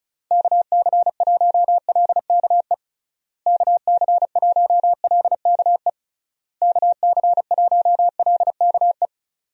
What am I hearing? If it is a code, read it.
Morse code, heard as a keyed tone, about 36 wpm, 700 Hz. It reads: KC1LKE KC1LKE KC1LKE
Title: USA Call Signs - 6 Characters - Repeated 3x - 35wpm